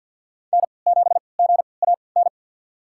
Morse code, heard as a keyed tone, about 53 wpm, spaced as for 29 wpm, 700 Hz. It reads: N7GAN
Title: USA Call Signs - 5 Characters - 3x Character Spacing - 50wpm